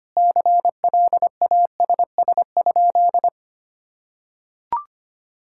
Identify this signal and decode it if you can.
Morse code, heard as a keyed tone, about 25 wpm, 700 Hz. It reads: CLASS?
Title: Sentences from Top 1000 Words - Review Part 6 - 25wpm